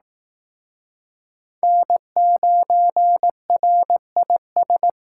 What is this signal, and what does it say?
Morse code, heard as a keyed tone, about 19 wpm, 700 Hz. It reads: N9RIS